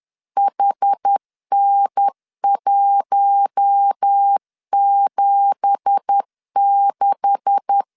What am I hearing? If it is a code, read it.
Morse code, heard as a keyed tone, about 10 wpm, 800 Hz. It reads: HN176